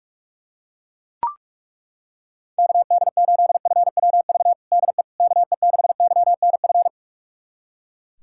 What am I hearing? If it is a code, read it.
Morse code, heard as a keyed tone, about 41 wpm, 700 Hz. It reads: KD8FWV DE KE6YNF
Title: USA Callsign DE Callsign 45wpm